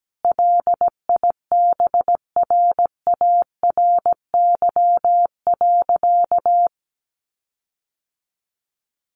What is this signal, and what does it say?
Morse code, heard as a keyed tone, about 17 wpm, 700 Hz. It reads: LIBRARY.
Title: Sentences from Top 1300 Words 17wpm